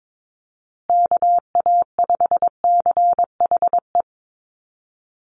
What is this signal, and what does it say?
Morse code, heard as a keyed tone, about 22 wpm, 700 Hz. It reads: KA5CHE
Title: USA Call Signs Repeated 3x 22wpm